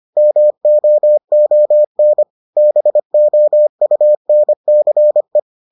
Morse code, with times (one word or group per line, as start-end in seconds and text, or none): 0.2-2.2 MOON
2.6-5.4 BOUNCE